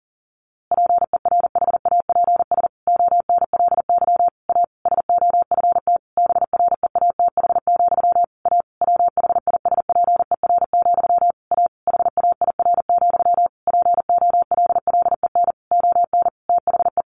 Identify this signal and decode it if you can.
Morse code, heard as a keyed tone, about 40 wpm, 700 Hz. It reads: PERHAPS ONLY A SOFT BREATH, A WHISPER, A HAIR, POLLEN ON THE
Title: A Sound Of Thunder 40wpm